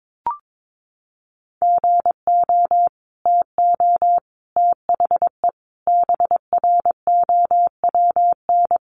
Morse code, heard as a keyed tone, about 22 wpm, 700 Hz. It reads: GO TO THE BROWN
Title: Sentences from Top 1200 Words 22wpm